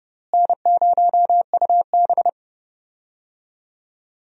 Morse code, heard as a keyed tone, about 30 wpm, 700 Hz. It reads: N0UB
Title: USA Call Signs - 4 Characters - Repeated 2x - 30wpm